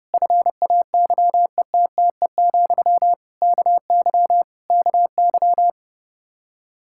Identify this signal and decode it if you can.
Morse code, heard as a keyed tone, about 32 wpm, 700 Hz. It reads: FAYETTE, KY KY
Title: Practice QSO - Exchange 1 - Beginning - 30wpm